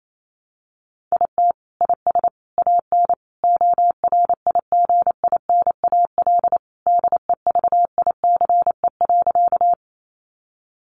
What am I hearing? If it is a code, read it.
Morse code, heard as a keyed tone, about 28 wpm, 700 Hz. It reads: IT IS AN ORIGINAL DEVICE.
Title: Sentences from Top 1500 Words 28wpm